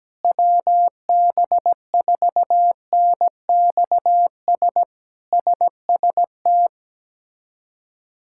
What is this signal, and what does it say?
Morse code, heard as a keyed tone, about 17 wpm, 700 Hz. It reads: WB4NXS SST
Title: K1USN SST QSO Practice - 17wpm